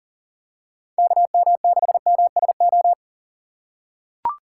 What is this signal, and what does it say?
Morse code, heard as a keyed tone, about 40 wpm, 700 Hz. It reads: KM6MSO
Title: USA Call Signs - 6x Character Spacing - 40wpm